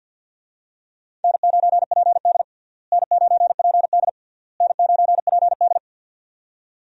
Morse code, heard as a keyed tone, about 52 wpm, 700 Hz. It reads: N9PD N9PD N9PD
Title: USA Call Signs - 4 Characters - Repeated 3x - 50wpm